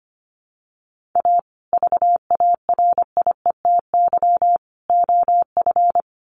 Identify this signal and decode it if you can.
Morse code, heard as a keyed tone, about 25 wpm, 700 Hz. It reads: A VARIETY OF
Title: Sets of 3 Words 25wpm